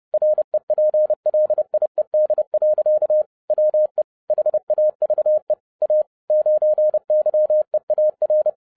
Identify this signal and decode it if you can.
Morse code, heard as a keyed tone, about 30 wpm, 610 Hz. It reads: REPLIED. WE HAVE A 9YEAR